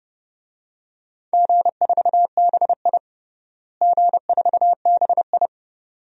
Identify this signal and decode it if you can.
Morse code, heard as a keyed tone, about 30 wpm, 700 Hz. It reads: G4BI G4BI
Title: DXCC-100 Call Signs - 4 Characters - Repeated 2x - 30wpm